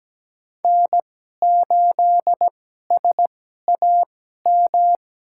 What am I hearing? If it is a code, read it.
Morse code, heard as a keyed tone, about 17 wpm, spaced as for 12 wpm, 700 Hz. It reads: N8SAM
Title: USA Call Signs - 5 Characters - 2x Character Spacing - 17wpm